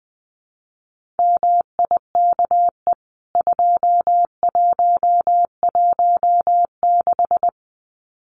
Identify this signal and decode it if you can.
Morse code, heard as a keyed tone, about 20 wpm, 700 Hz. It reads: MIKE 2116
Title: CWT Exchange 20wpm